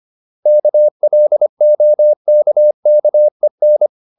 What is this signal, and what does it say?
Morse code, heard as a keyed tone, about 25 wpm, 600 Hz. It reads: KLOKKEN
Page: Ferie 25 wpm